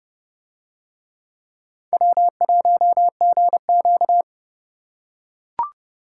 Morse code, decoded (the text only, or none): W1GQ